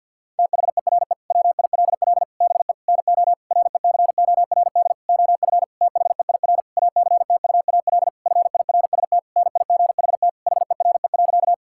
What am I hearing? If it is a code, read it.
Morse code, heard as a keyed tone, about 50 wpm, 700 Hz. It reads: THERE WILL BE NO RECORD OF THEIR ACTUAL FIRST NIGHT HERE.
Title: Sentences from Top 1000 Words - Review Part 5 - 50wpm